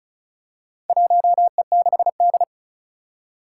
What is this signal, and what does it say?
Morse code, heard as a keyed tone, about 34 wpm, 700 Hz. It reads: 1E6D
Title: Four Challenging Letter-Number Combinations 35wpm